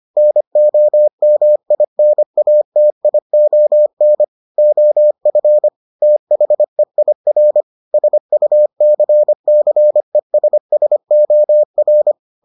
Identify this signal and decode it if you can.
Morse code, heard as a keyed tone, about 25 wpm, 600 Hz. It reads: NOMINATION OF THEIR SUCCESSOR